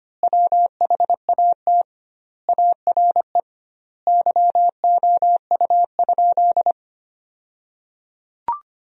Morse code, heard as a keyed tone, about 25 wpm, 700 Hz. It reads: WHAT ARE YOU?
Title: 2x Word-Spacing - Sentences from Top 100 Words - 25wpm